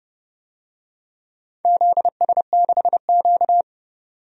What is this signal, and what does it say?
Morse code, heard as a keyed tone, about 30 wpm, 700 Hz. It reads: ZS6Q
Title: DXCC-100 Call Signs - 4 Characters - 30wpm